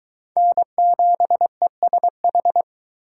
Morse code, heard as a keyed tone, about 22 wpm, 700 Hz. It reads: N7ESH